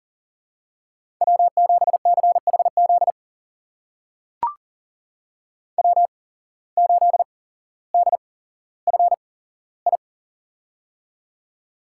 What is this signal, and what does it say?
Morse code, heard as a keyed tone, about 40 wpm, spaced as for 17 wpm, 700 Hz. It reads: W7CHZ W8DFI
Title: USA Call Signs - 5 Characters - 8x Character Spacing - 40wpm